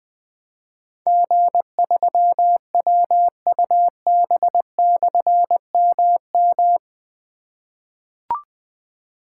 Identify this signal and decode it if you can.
Morse code, heard as a keyed tone, about 21 wpm, 700 Hz. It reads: G3WUB/MM
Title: Difficult Call Signs 20wpm